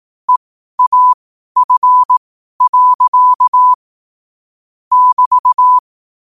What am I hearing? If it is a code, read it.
Morse code, heard as a keyed tone, about 18 wpm, spaced as for 12 wpm, 1000 Hz. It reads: EAF. <BT>
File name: qotdfiles / 12wpm